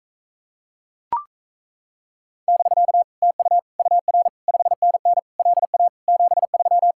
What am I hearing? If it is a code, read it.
Morse code, heard as a keyed tone, about 37 wpm, 700 Hz. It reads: <BK> TU UR 5NN LA 73